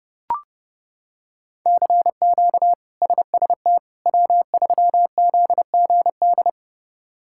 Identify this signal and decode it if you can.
Morse code, heard as a keyed tone, about 30 wpm, 700 Hz. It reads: CQ SST W3ZGD
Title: K1USN SST QSO Practice - 30wpm